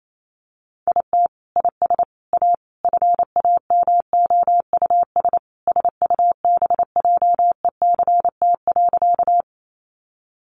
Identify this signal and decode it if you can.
Morse code, heard as a keyed tone, about 28 wpm, 700 Hz. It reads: IT IS A FAMOUS SUBJECT.
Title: Sentences from Top 1500 Words 28wpm